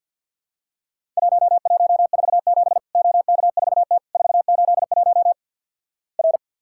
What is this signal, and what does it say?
Morse code, heard as a keyed tone, about 50 wpm, 700 Hz. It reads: 1147 OK4T 481 R
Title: Worked All Europe DX Contest - QTC Exchange - 50wpm